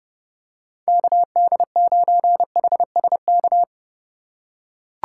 Morse code, heard as a keyed tone, about 28 wpm, 700 Hz. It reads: KD9HSK